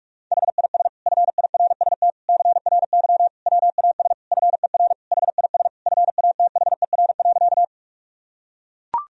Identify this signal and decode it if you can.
Morse code, heard as a keyed tone, about 41 wpm, 700 Hz. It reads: HIS FIRST CRY WAS FER HIS FATHER.
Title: Sentences from Top 500 Words and Abbreviations 45wpm